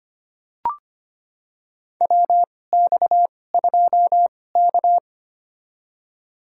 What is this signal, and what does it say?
Morse code, heard as a keyed tone, about 25 wpm, spaced as for 18 wpm, 700 Hz. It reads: WX2K
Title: USA Call Signs - 4 Characters - 2x Character Spacing - 25wpm